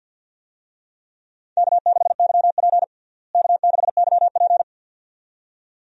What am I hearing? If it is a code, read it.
Morse code, heard as a keyed tone, about 52 wpm, 700 Hz. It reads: K6YP K6YP
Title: USA Call Signs - 4 Characters - Repeated 2x - 50wpm